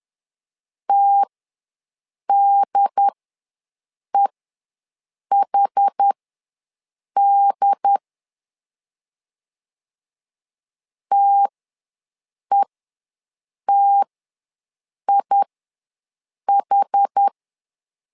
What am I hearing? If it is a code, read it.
Morse code, heard as a keyed tone, about 10 wpm, spaced as for 6 wpm, 800 Hz. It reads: TDEHD TETIH
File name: LEZIONI / 2 UHDN / 2 UHD ws 10wpm 2058